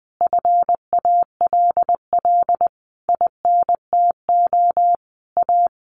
Morse code, heard as a keyed tone, about 20 wpm, 700 Hz. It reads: FALL INTO A
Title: A Sound Of Thunder 20wpm